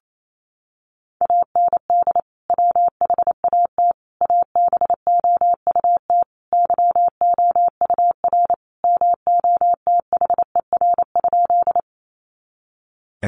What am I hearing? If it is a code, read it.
Morse code, heard as a keyed tone, about 28 wpm, 700 Hz. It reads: AND WHAT ABOUT YOUR MOTHER?
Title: Sentences from Top 200 Words 28wpm